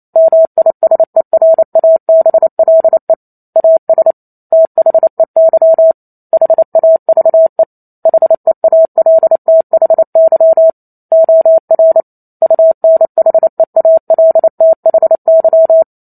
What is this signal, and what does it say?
Morse code, heard as a keyed tone, about 28 wpm, 650 Hz. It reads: MISERABLE AS THEY HAVE HEALTHY OR UNHEALTHY